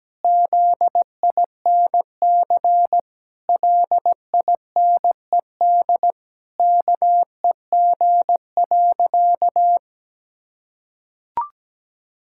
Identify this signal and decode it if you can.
Morse code, heard as a keyed tone, about 17 wpm, 700 Hz. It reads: ZINC LINED KEG.
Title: Pangram Challenge v2 - 17wpm